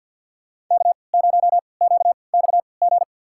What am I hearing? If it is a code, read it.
Morse code, heard as a keyed tone, about 52 wpm, spaced as for 28 wpm, 700 Hz. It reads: K0QXG
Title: USA Call Signs - 5 Characters - 3x Character Spacing - 50wpm